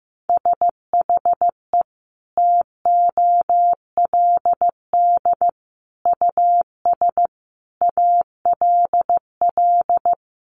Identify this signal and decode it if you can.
Morse code, heard as a keyed tone, about 15 wpm, 700 Hz. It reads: SHE TOLD US ALL
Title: Sentences from Top 1200 Words 15wpm